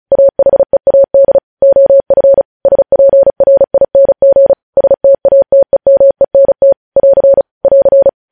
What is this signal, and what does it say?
Morse code, heard as a keyed tone, about 36 wpm, 550 Hz. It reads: AHEAD OF SPRING STATEMENT <AR> <AR>
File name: podcast / News headlines-1x-35x35